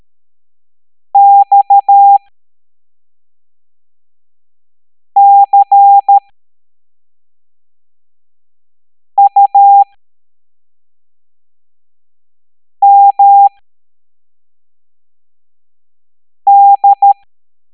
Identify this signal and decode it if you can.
Morse code, heard as a keyed tone, about 13 wpm, spaced as for 3 wpm, 800 Hz. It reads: XCUMD